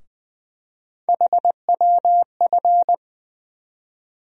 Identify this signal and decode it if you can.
Morse code, heard as a keyed tone, about 20 wpm, 700 Hz. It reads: HWF
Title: USA Call Sign Postfixes - 3 Letters 20wpm